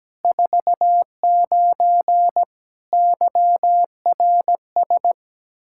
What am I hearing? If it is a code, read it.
Morse code, heard as a keyed tone, about 17 wpm, 700 Hz. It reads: 49 YRS
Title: K1USN SST QSO Practice 17wpm